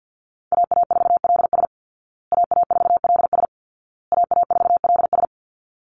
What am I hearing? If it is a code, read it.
Morse code, heard as a keyed tone, about 49 wpm, 700 Hz. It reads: AA4LS AA4LS AA4LS
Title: Call Signs Repeated 2x - 50wpm